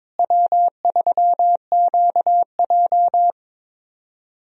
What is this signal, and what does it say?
Morse code, heard as a keyed tone, about 22 wpm, 700 Hz. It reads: W3QJ
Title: USA Call Signs - 4 Characters - Repeated 3x - 22wpm